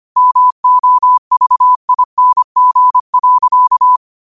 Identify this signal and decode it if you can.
Morse code, heard as a keyed tone, about 25 wpm, 1000 Hz. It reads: MOVING.